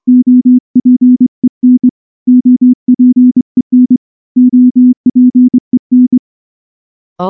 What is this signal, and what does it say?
Morse code, heard as a keyed tone, about 25 wpm, 250 Hz. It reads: OPEN OPEN OPEN
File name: morse / morse-training-25WPM